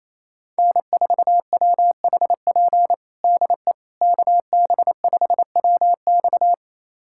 Morse code, heard as a keyed tone, about 28 wpm, 700 Hz. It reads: N4WHP DE KB5WX
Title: USA Callsign DE Callsign - 5 Characters - 28wpm